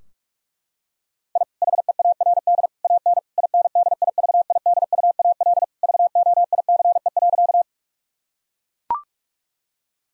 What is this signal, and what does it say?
Morse code, heard as a keyed tone, about 41 wpm, 700 Hz. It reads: I HEARD AN INDIVIDUAL VOICE.
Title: Sentences from Top 1000 Words - Review Part 4 - 45wpm